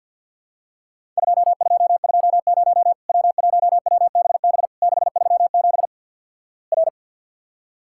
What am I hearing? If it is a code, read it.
Morse code, heard as a keyed tone, about 48 wpm, 700 Hz. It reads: 2220 W1WBB 637 R
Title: Worked All Europe DX Contest - QTC Exchange - 50wpm